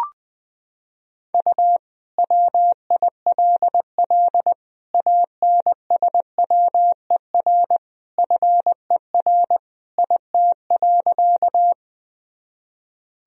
Morse code, heard as a keyed tone, about 20 wpm, 700 Hz. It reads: U WILL ANSWER FER IT.
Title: Sentences from Top 300 Words and Abbreviations 20wpm